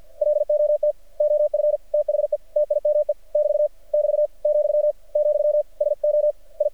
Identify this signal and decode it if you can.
Morse code, heard as a keyed tone, about 48 wpm, 600 Hz. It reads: LOT OF THE TIME <BT> <BT> , , SO I
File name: archived / CRSnet